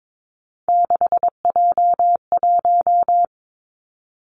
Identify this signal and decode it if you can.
Morse code, heard as a keyed tone, about 22 wpm, 700 Hz. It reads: 6J1
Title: Three Challenging Letter-Number Combinations 22wpm